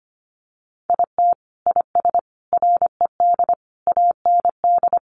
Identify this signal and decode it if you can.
Morse code, heard as a keyed tone, about 25 wpm, 700 Hz. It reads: IT IS RED AND
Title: Additional Easy Phrases 25wpm